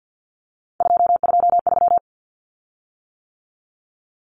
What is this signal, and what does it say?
Morse code, heard as a keyed tone, about 50 wpm, 700 Hz. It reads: 223
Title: Common Cut Numbers - 3-Digit 50wpm